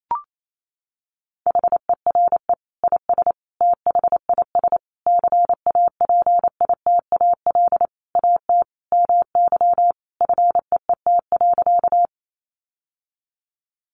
Morse code, heard as a keyed tone, about 28 wpm, 700 Hz. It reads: HERE IS THIS CAPITAL AT MY FEET.
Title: Sentences from Top 1000 Words 28wpm